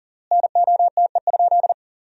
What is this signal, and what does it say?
Morse code, heard as a keyed tone, about 40 wpm, 700 Hz. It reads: NOTE?
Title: 3x Word-Spacing - Sentences from Top 500 Words - 40wpm